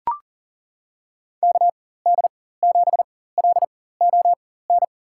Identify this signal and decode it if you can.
Morse code, heard as a keyed tone, about 40 wpm, spaced as for 19 wpm, 700 Hz. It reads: KD7LON
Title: USA Call Signs - 4x Character Spacing - 40wpm